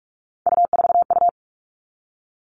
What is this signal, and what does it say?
Morse code, heard as a keyed tone, about 44 wpm, 700 Hz. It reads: U4U
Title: All Cut Numbers - 3-Digit 45wpm